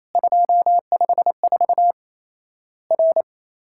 Morse code, heard as a keyed tone, about 28 wpm, 700 Hz. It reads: 254 R
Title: Worked All Europe DX Contest - QTC Exchange - 28wpm